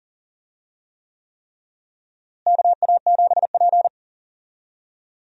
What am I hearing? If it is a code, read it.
Morse code, heard as a keyed tone, about 40 wpm, 700 Hz. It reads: KA7P
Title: USA Call Signs - 4 Characters - Repeated 3x - 40wpm